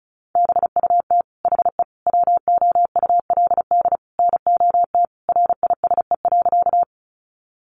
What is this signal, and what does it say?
Morse code, heard as a keyed tone, about 34 wpm, 700 Hz. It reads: BUT HE WOULD NOT RISE.
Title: Sentences from Top 900 Words 35wpm